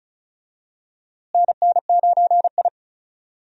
Morse code, heard as a keyed tone, about 36 wpm, 700 Hz. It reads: NN9I
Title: USA Call Signs - 4 Characters - Repeated 2x - 35wpm